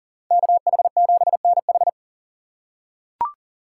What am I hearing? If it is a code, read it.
Morse code, heard as a keyed tone, about 40 wpm, 700 Hz. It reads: KH7NH
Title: USA Call Signs - 6x Character Spacing - 40wpm